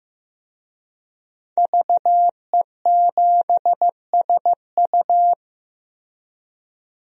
Morse code, heard as a keyed tone, about 15 wpm, 700 Hz. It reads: VE7SU
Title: DXCC-100 Call Signs - 5 Characters - Repeated 2x - 15wpm